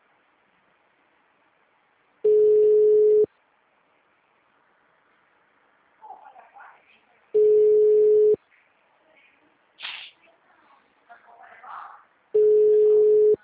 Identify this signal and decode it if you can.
Ringback tone (Europe, 425 Hz), with speech.
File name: out- -2002- - - .6267